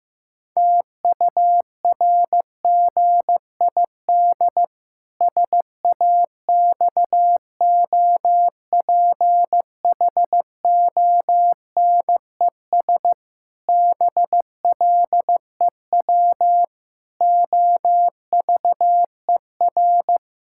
Morse code, heard as a keyed tone, about 15 wpm, 700 Hz. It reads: TURGID SAXOPHONES BLEW OVER
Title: Pangram Challenge v2 - 15wpm